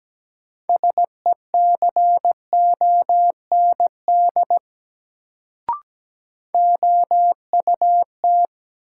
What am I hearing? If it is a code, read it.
Morse code, heard as a keyed tone, about 17 wpm, 700 Hz. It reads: SECOND OUT
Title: Sets of 2 Words - New Word Drill - 17wpm